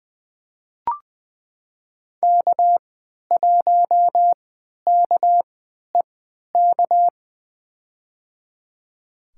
Morse code, heard as a keyed tone, about 20 wpm, spaced as for 11 wpm, 700 Hz. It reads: K1KEK
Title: USA Call Signs - 3x Character Spacing - 20wpm